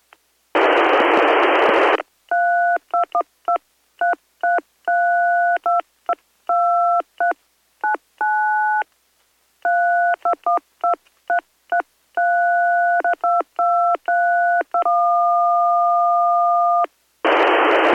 DTMF, heard as telephone touch tones, tones of about 140 ms. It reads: 3212333222399 321233322321